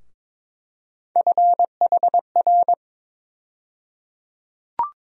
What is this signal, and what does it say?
Morse code, heard as a keyed tone, about 22 wpm, 700 Hz. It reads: FHR
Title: USA Call Sign Postfixes - 3 Letters 22wpm